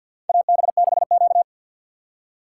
Morse code, heard as a keyed tone, about 50 wpm, 700 Hz. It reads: AB6Q
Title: USA Call Signs - 2x Character Spacing - 50wpm